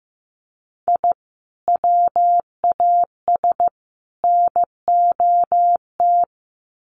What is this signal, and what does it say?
Morse code, heard as a keyed tone, about 14 wpm, 700 Hz. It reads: I WAS NOT
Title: Sets of 3 Words 15wpm